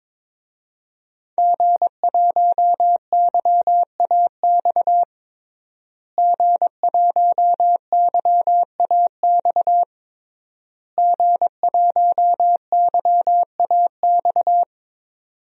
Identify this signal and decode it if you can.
Morse code, heard as a keyed tone, about 22 wpm, 700 Hz. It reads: G1YAX G1YAX G1YAX
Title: DXCC-100 Call Signs - 5 Characters - Repeated 3x - 22wpm